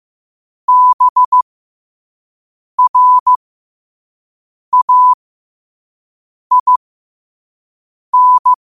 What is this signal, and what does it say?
Morse code, heard as a keyed tone, about 15 wpm, spaced as for 5 wpm, 1000 Hz. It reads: BRAIN
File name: qotdfiles / 5wpm